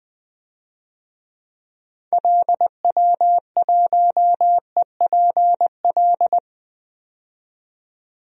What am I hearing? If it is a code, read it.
Morse code, heard as a keyed tone, about 20 wpm, 700 Hz. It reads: LW1EPL
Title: DXCC-100 Call Signs - 6 Characters - 20wpm